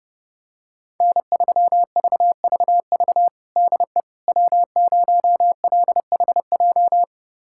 Morse code, heard as a keyed tone, about 32 wpm, 700 Hz. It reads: N3VVV DE W0LHJ
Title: USA Callsign DE Callsign - 5 Characters - 30wpm